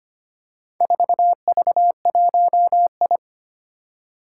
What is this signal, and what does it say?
Morse code, heard as a keyed tone, about 25 wpm, 700 Hz. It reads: 4V1I
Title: Four Challenging Letter-Number Combinations 25wpm